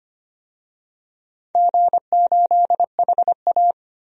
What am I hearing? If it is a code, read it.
Morse code, heard as a keyed tone, about 25 wpm, 690 Hz. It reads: G8HA
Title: DXCC-100 Call Signs - 4 Characters - 25wpm